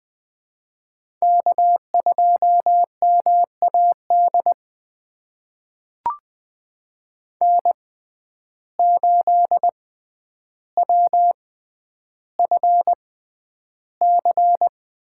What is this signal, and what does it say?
Morse code, heard as a keyed tone, about 20 wpm, spaced as for 10 wpm, 700 Hz. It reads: K2MAD N8WFC
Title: USA Call Signs - 5 Characters - 6x Character Spacing - 20wpm